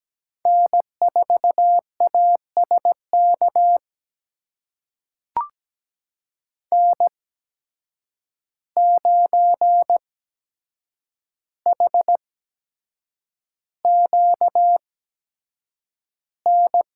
Morse code, heard as a keyed tone, about 17 wpm, spaced as for 7 wpm, 700 Hz. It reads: N4ASK N9HQN
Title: USA Call Signs - 5 Characters - 8x Character Spacing - 17wpm